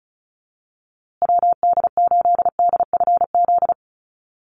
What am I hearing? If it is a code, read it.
Morse code, heard as a keyed tone, about 34 wpm, 700 Hz. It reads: WD8DFZ
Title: USA Call Signs Repeated 2x 35wpm